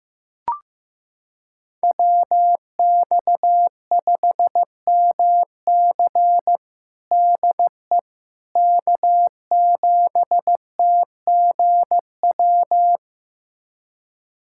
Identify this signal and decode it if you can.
Morse code, heard as a keyed tone, about 15 wpm, 700 Hz. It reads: WX5MC DE K7TGW
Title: USA Callsign DE Callsign - 5 Characters - 15wpm